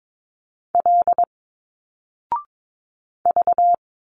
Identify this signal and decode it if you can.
Morse code, heard as a keyed tone, about 22 wpm, 700 Hz. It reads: L V
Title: Single Letters 22wpm